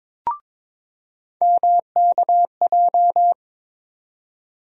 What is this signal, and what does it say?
Morse code, heard as a keyed tone, about 21 wpm, 700 Hz. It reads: MKJ
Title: USA Call Sign Postfixes - 3 Letters 22wpm